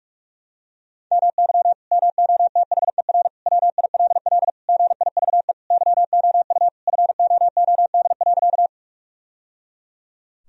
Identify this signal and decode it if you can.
Morse code, heard as a keyed tone, about 42 wpm, 700 Hz. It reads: MY MOTHER WILL GIVE YOU FOOD.
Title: Sentences from Top 1000 Words - Review Part 1 - 45wpm